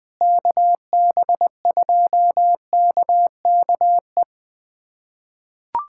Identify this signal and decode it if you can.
Morse code, heard as a keyed tone, about 20 wpm, 700 Hz. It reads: KB2KKE
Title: USA Callsign DE Callsign - 6 Characters - 20wpm